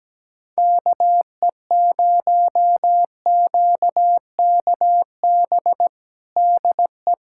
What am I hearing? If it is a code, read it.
Morse code, heard as a keyed tone, about 17 wpm, 700 Hz. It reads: KE0QKB DE
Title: USA Callsign DE Callsign - 6 Characters - 17wpm